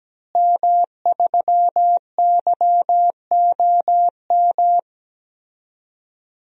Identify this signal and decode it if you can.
Morse code, heard as a keyed tone, about 17 wpm, 700 Hz. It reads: M3YOM
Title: DXCC-100 Call Signs - 5 Characters - Repeated 3x - 17wpm